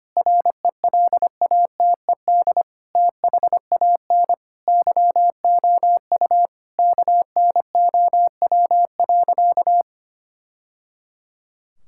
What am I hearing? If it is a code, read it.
Morse code, heard as a keyed tone, about 25 wpm, 700 Hz. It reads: RELATED THAN YOU KNOW.
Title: Sentences from Top 1000 Words - Review Part 5 - 25wpm